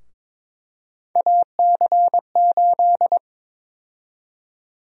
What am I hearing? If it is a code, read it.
Morse code, heard as a keyed tone, about 22 wpm, 700 Hz. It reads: AC8
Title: USA Call Sign Prefixes - Repeated 2x - 22wpm